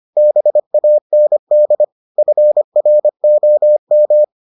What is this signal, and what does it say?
Morse code, heard as a keyed tone, about 25 wpm, 600 Hz. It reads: BAND FROM